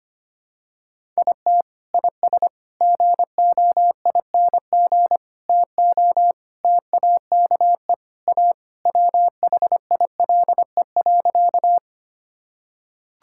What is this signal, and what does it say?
Morse code, heard as a keyed tone, about 25 wpm, 700 Hz. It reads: IT IS GOING TO TAKE A WHILE.
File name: practice / 25wpm / Sentences with 4-Word Phrases - 25wpm